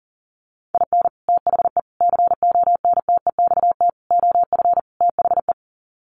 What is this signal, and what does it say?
Morse code, heard as a keyed tone, about 40 wpm, 700 Hz. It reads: IN THE CONTEXT OF THE
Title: Sets of 5 Words 40wpm